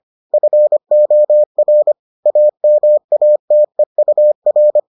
Morse code, heard as a keyed tone, about 25 wpm, 600 Hz. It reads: FOR AMATEUR